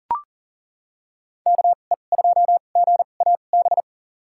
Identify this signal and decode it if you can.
Morse code, heard as a keyed tone, about 40 wpm, spaced as for 29 wpm, 700 Hz. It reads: KE2GAB
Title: USA Call Signs - 6 Characters - 2x Character Spacing - 40wpm